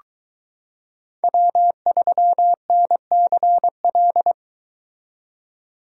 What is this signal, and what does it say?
Morse code, heard as a keyed tone, about 22 wpm, 700 Hz. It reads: W3NCL